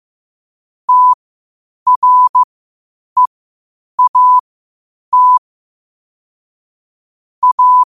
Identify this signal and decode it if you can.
Morse code, heard as a keyed tone, about 15 wpm, spaced as for 8 wpm, 1000 Hz. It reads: TREAT A
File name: qotdfiles / 8wpm